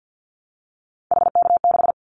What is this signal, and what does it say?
Morse code, heard as a keyed tone, about 49 wpm, 700 Hz. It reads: HK6
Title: Callsign Prefixes for Top 100 DXCC - 50wpm